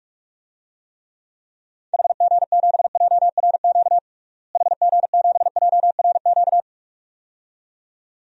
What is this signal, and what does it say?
Morse code, heard as a keyed tone, about 46 wpm, 700 Hz. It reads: HG7JRQ HG7JRQ
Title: DXCC-100 Call Signs - 6 Characters - Repeated 2x - 45wpm